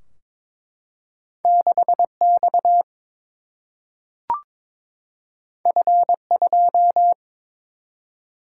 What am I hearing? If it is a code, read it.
Morse code, heard as a keyed tone, about 22 wpm, 700 Hz. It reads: 6 X F 2
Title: Single-Letter Single-Number 22wpm